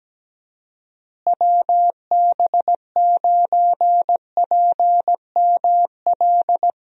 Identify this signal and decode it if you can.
Morse code, heard as a keyed tone, about 17 wpm, 700 Hz. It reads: WB9PML
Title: USA Call Signs - 6 Characters - Repeated 2x - 17wpm